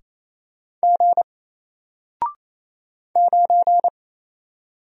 Morse code, heard as a keyed tone, about 27 wpm, 700 Hz. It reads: G 9
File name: Single-Letter-Number-28wpm